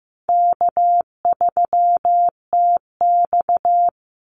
Morse code, heard as a keyed tone, about 15 wpm, 700 Hz. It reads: K3TX
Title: Easy Call Signs Repeated 3x - 15wpm